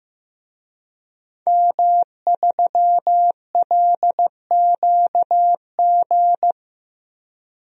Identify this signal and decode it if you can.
Morse code, heard as a keyed tone, about 15 wpm, 700 Hz. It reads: M3LQG
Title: DXCC-100 Call Signs - 5 Characters - Repeated 2x - 15wpm